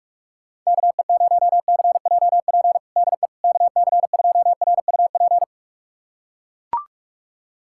Morse code, heard as a keyed tone, about 41 wpm, 690 Hz. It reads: KE0CJP DE KC2RUP
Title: USA Callsign DE Callsign - 6 Characters - 45wpm